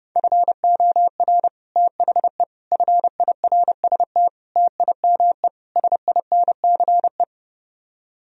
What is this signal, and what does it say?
Morse code, heard as a keyed tone, about 30 wpm, 700 Hz. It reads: FOR THE FIRST TIME SINCE